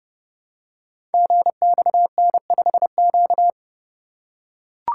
Morse code, heard as a keyed tone, about 30 wpm, 700 Hz. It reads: GXN5Q
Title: Code Groups - 5 Letter:Numbers - 30wpm